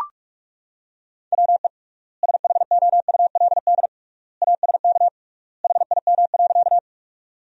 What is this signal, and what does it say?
Morse code, heard as a keyed tone, about 44 wpm, 700 Hz. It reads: WE SHOULD ASK HIM.
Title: 3x Word-Spacing - Sentences from Top 300 Words - 45wpm